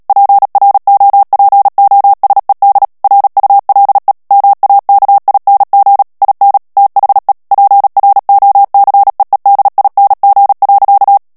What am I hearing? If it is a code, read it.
Morse code, heard as a keyed tone, about 37 wpm, 800 Hz. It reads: PROPOSED RULE MAKING IN THE PROCEEDING.
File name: morse mp3 / ANGLAIS texte clair 3 vitesse 35 mots minute